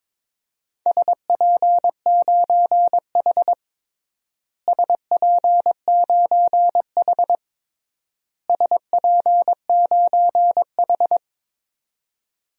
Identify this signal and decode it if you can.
Morse code, heard as a keyed tone, about 22 wpm, 700 Hz. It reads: SP9H SP9H SP9H
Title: DXCC-100 Call Signs - 4 Characters - Repeated 3x - 22wpm